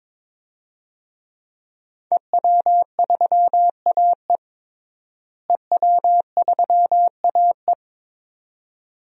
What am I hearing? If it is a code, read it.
Morse code, heard as a keyed tone, about 22 wpm, 700 Hz. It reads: EW3AE EW3AE
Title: DXCC-100 Call Signs - 5 Characters - Repeated 2x - 22wpm